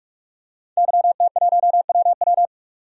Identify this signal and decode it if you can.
Morse code, heard as a keyed tone, about 44 wpm, 700 Hz. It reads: YT1WW
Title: DXCC-100 Call Signs - 5 Characters - 45wpm